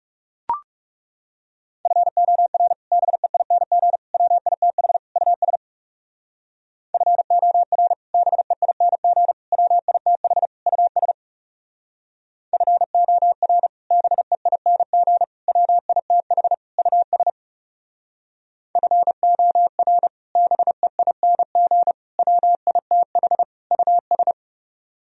Morse code, heard as a keyed tone, about 37 wpm, 700 Hz. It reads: FOR BEING WITH US FOR BEING WITH US FOR BEING WITH US FOR BEING WITH US
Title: Speed-Racing - Sets of 4 Words 30wpm